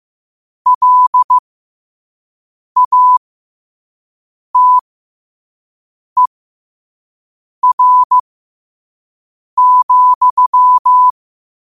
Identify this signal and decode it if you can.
Morse code, heard as a keyed tone, about 15 wpm, spaced as for 5 wpm, 1000 Hz. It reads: LATER,
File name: qotdfiles / 5wpm